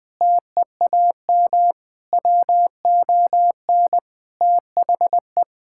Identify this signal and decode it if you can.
Morse code, heard as a keyed tone, about 21 wpm, 700 Hz. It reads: TEAM WON THE
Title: Sentences with 8-Word Phrases - Spoken Beginning - 20wpm